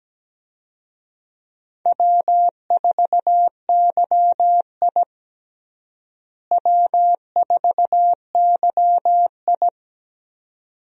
Morse code, heard as a keyed tone, about 17 wpm, 700 Hz. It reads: W4YI W4YI
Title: USA Call Signs - 4 Characters - Repeated 2x - 17wpm